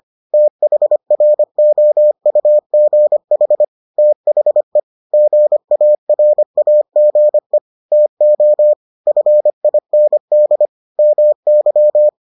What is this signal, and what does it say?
Morse code, heard as a keyed tone, about 25 wpm, 600 Hz. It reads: THROUGH THE GARAGE TO FIND MY